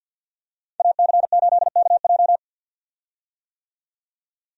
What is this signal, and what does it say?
Morse code, heard as a keyed tone, about 48 wpm, 700 Hz. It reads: AC8KJ
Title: USA Call Signs - 5 Characters - 50wpm